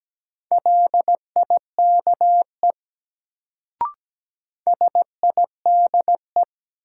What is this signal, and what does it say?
Morse code, heard as a keyed tone, about 17 wpm, 700 Hz. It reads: LIKE SIDE
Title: Sets of 2 Words - New Word Drill - 17wpm